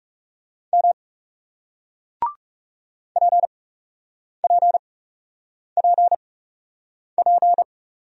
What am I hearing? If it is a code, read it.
Morse code, heard as a keyed tone, about 38 wpm, spaced as for 8 wpm, 700 Hz. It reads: M PPPP